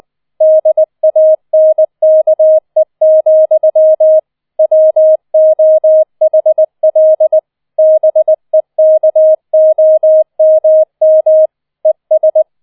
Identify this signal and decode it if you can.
Morse code, heard as a keyed tone, about 20 wpm, 630 Hz. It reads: DANKE, WOHL BEKOMM ES